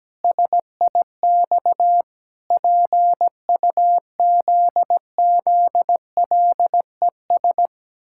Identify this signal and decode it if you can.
Morse code, heard as a keyed tone, about 17 wpm, 700 Hz. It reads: SIX PUZZLES
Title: Pangram Challenge v2 - 17wpm